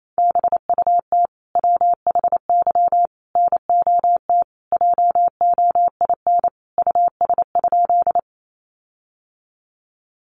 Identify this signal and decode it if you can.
Morse code, heard as a keyed tone, about 28 wpm, 700 Hz. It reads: BUT WHY NOT JOIN US?
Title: Sentences from Top 1400 Words 28wpm